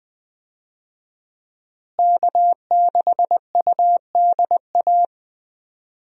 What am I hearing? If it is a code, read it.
Morse code, heard as a keyed tone, about 20 wpm, 700 Hz. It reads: K6UDA
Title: USA Call Signs - 5 Characters - Repeated 2x - 20wpm